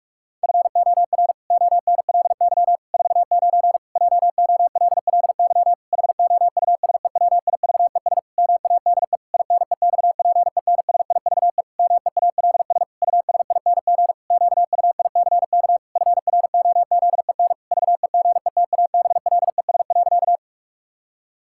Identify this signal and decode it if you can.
Morse code, heard as a keyed tone, about 46 wpm, 700 Hz. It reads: FOR ONLY 49 JOLLY HOUSEWIVES MADE INEXPENSIVE MEALS USING QUICK FROZEN VEGETABLES.
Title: Pangram Challenge v2 - 45wpm